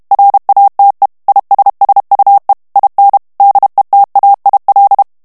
Morse code, heard as a keyed tone, about 32 wpm, 800 Hz. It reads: RATE ISSUE IN DETAIL
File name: ANGLAIS texte clair 3 vitesse 30 mots minute